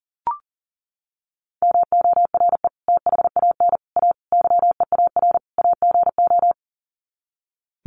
Morse code, heard as a keyed tone, about 40 wpm, 700 Hz. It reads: MORE THAN A YEAR AGO
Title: Sets of 5 Words Encore2 40wpm